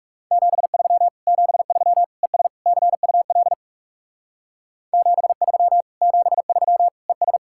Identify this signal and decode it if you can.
Morse code, heard as a keyed tone, about 37 wpm, 700 Hz. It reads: 73 73 ES CUL 73 73 ES
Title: Speed-Racing Common QSO Phrases - 30wpm